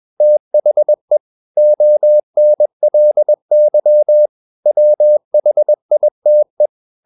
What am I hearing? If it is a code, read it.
Morse code, heard as a keyed tone, about 21 wpm, 600 Hz. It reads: THE ONLY WHITE
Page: CW med Gnister